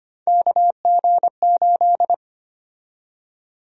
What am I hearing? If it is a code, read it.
Morse code, heard as a keyed tone, about 25 wpm, 700 Hz. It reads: KG8
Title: USA Call Sign Prefixes - Repeated 2x - 25wpm